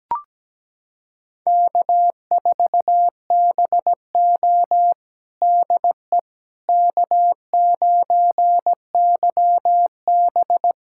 Morse code, heard as a keyed tone, about 17 wpm, 700 Hz. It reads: K4BO DE K9YB
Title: USA Callsign DE Callsign - 4 Characters - 17wpm